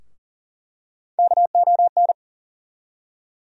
Morse code, heard as a keyed tone, about 40 wpm, 700 Hz. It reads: KON
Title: USA Call Sign Postfixes - 3 Letters 40wpm